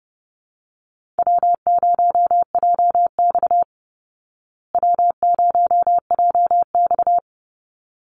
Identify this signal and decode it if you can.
Morse code, heard as a keyed tone, about 30 wpm, 700 Hz. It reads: W0JX W0JX
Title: Easy Call Signs Repeated 2x - 30wpm